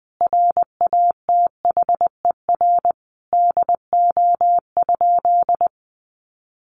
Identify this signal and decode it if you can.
Morse code, heard as a keyed tone, about 20 wpm, 700 Hz. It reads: RATHER DO?
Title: Sentences from Top 700 Words 20wpm